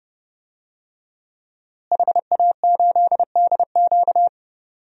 Morse code, heard as a keyed tone, about 30 wpm, 700 Hz. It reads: HA8DQ
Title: DXCC-100 Call Signs - 5 Characters - Repeated 3x - 30wpm